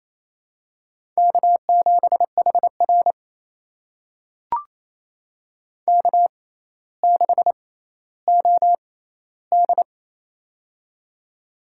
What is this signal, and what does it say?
Morse code, heard as a keyed tone, about 27 wpm, spaced as for 14 wpm, 700 Hz. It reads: K7HR K6OD
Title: USA Call Signs - 4 Characters - 6x Character Spacing - 28wpm